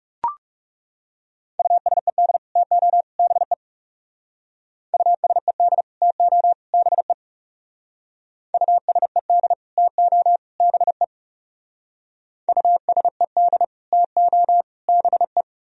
Morse code, heard as a keyed tone, about 40 wpm, 700 Hz. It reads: USED TO BE USED TO BE USED TO BE USED TO BE